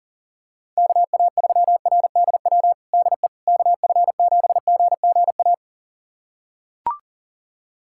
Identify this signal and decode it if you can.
Morse code, heard as a keyed tone, about 40 wpm, 700 Hz. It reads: KA3RDW DE KF7GGA
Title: USA Callsign DE Callsign 40wpm